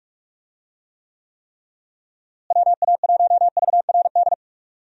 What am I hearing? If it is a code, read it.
Morse code, heard as a keyed tone, about 45 wpm, 700 Hz. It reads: WA1VRD
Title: USA Call Signs - 6 Characters - Repeated 3x - 45wpm